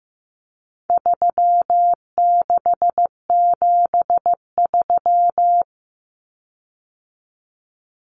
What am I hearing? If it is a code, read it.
Morse code, heard as a keyed tone, about 15 wpm, 700 Hz. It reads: 3673
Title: Common Cut Numbers - 4-Digit 15wpm